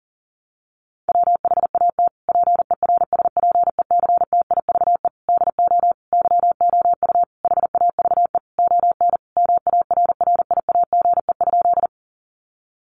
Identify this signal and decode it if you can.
Morse code, heard as a keyed tone, about 40 wpm, 700 Hz. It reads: WHAT PERSPECTIVE DO YOU HAVE ON MARRIAGE?
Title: Sentences from Top 1200 Words 40wpm